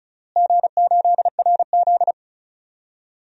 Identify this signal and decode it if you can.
Morse code, heard as a keyed tone, about 36 wpm, 700 Hz. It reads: G8RZ
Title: DXCC-100 Call Signs - 4 Characters - Repeated 3x - 35wpm